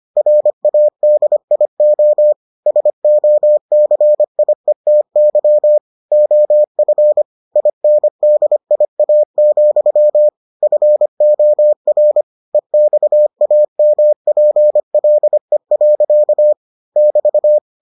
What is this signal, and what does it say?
Morse code, heard as a keyed tone, about 25 wpm, 600 Hz. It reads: RADIO SOCIETY OF INDIA, FOR EXAMPLE. <BT>